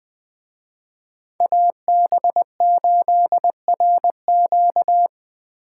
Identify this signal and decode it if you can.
Morse code, heard as a keyed tone, about 19 wpm, 700 Hz. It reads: AB8RQ
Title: Speed Racing - USA Call Signs 17wpm